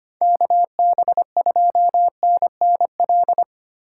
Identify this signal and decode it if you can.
Morse code, heard as a keyed tone, about 25 wpm, 700 Hz. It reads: KB2NNL
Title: USA Call Signs - 6 Characters - 4x Character Spacing - 25wpm